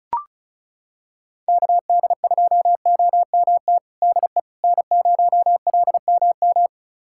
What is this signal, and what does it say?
Morse code, heard as a keyed tone, about 36 wpm, 700 Hz. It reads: KD2OMT DE N0LMM
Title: USA Callsign DE Callsign 35wpm